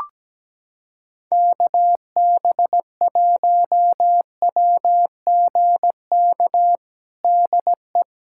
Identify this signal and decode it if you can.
Morse code, heard as a keyed tone, about 17 wpm, 700 Hz. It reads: KB1WGK DE
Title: USA Callsign DE Callsign - 6 Characters - 17wpm